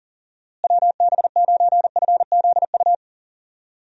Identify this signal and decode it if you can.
Morse code, heard as a keyed tone, about 37 wpm, 700 Hz. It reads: WB9FZU
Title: USA Call Signs - 6 Characters - Repeated 2x - 40wpm